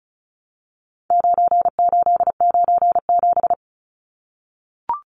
Morse code, heard as a keyed tone, about 34 wpm, 700 Hz. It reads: 9897
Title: Numbers - 4-Digit 35wpm